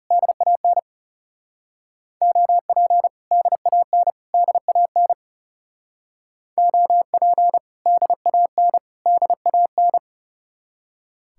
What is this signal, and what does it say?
Morse code, heard as a keyed tone, about 36 wpm, 700 Hz. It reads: DAN OP DAN DAN OP DAN DAN
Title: Speed-Racing Common QSO Phrases Encore - 30wpm